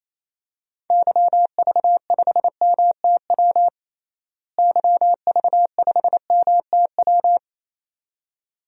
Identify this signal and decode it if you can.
Morse code, heard as a keyed tone, about 27 wpm, 700 Hz. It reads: YV5MTW YV5MTW
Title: DXCC-100 Call Signs - 6 Characters - Repeated 2x - 28wpm